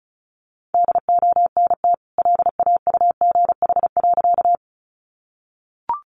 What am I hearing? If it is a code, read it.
Morse code, heard as a keyed tone, about 34 wpm, 700 Hz. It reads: DONT LAUGH.
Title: Sentences from Top 500 Words 35wpm